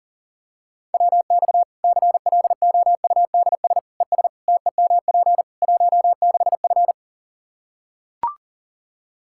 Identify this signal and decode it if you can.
Morse code, heard as a keyed tone, about 40 wpm, 700 Hz. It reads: WX CLOUDS ES TEMP 16F
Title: Practice QSO - Exchange 2 - Info - 40wpm